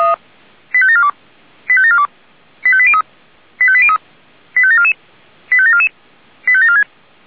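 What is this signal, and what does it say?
Five-tone selective call (ZVEI), heard as quick runs of five tones, tones of about 70 ms. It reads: 85742 85742 85803 85803 85744 85744 85746
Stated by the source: ZVEI five-tone selective call received on SDR.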